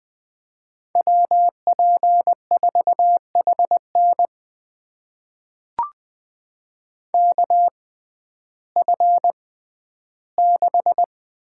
Morse code, heard as a keyed tone, about 20 wpm, spaced as for 12 wpm, 700 Hz. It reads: WP4HN KF6
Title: USA Call Signs - 5 Characters - 6x Character Spacing - 20wpm